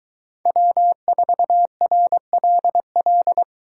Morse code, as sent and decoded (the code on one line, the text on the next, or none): .-- ....- .-. .-.. .-..
W4RLL